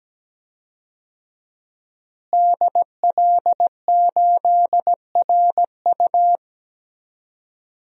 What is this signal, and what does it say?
Morse code, heard as a keyed tone, about 17 wpm, 700 Hz. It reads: DL8RU
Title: DXCC-100 Call Signs - 5 Characters - Repeated 2x - 17wpm